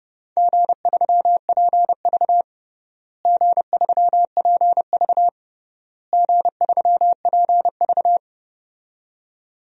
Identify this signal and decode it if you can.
Morse code, heard as a keyed tone, about 30 wpm, 700 Hz. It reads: G3PV G3PV G3PV
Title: DXCC-100 Call Signs - 4 Characters - Repeated 3x - 30wpm